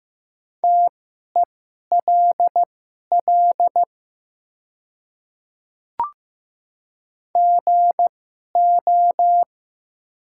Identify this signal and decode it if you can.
Morse code, heard as a keyed tone, about 15 wpm, spaced as for 11 wpm, 700 Hz. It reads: TELL GO
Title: 2x Character Spacing - Top 200 Words - 15wpm